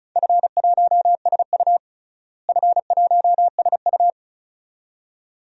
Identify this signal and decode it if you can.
Morse code, heard as a keyed tone, about 36 wpm, 700 Hz. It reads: F1SU F1SU
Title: DXCC-100 Call Signs - 4 Characters - Repeated 2x - 35wpm